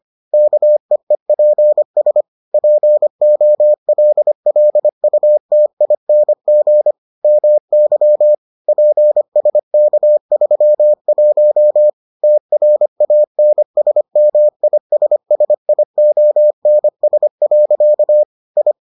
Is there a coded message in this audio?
Yes: Morse code from 0.3 to 18.7 s, heard as a keyed tone, about 25 wpm, 600 Hz. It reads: KEEPS POLLUTING MY PSK31 TRANSMISSIONS. I